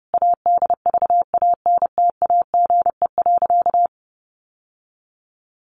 Morse code, heard as a keyed tone, about 30 wpm, 700 Hz. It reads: ADVANTAGE.
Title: Sentences from Top 900 Words 30wpm